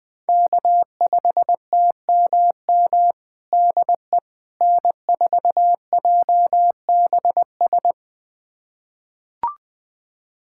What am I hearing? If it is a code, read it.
Morse code, heard as a keyed tone, about 21 wpm, 700 Hz. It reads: K5TMM DE N4JBS
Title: USA Callsign DE Callsign - 5 Characters - 20wpm